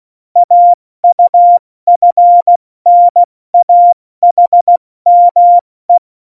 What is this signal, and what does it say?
Morse code, heard as a keyed tone, about 16 wpm, 700 Hz. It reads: AUFNAHME
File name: Morsekurs / Witze / Witz 26. Januar 2021, 70 F80